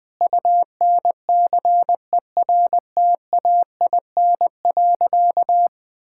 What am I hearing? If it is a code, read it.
Morse code, heard as a keyed tone, about 20 wpm, 700 Hz. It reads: UNCERTAIN.
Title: Sentences with 3-Word Phrases - Spoken Beginning - 20wpm